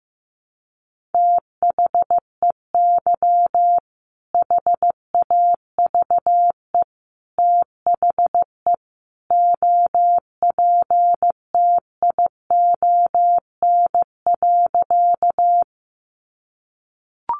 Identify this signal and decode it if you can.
Morse code, heard as a keyed tone, about 15 wpm, 700 Hz. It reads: THEY HAVE THE OPTION.
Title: Sentences from Top 1300 Words 15wpm